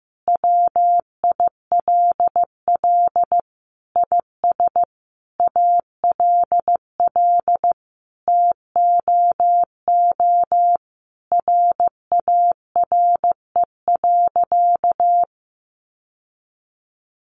Morse code, heard as a keyed tone, about 15 wpm, 700 Hz. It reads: WILL IS ALL TOO RARE.
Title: Sentences from Top 1500 Words 15wpm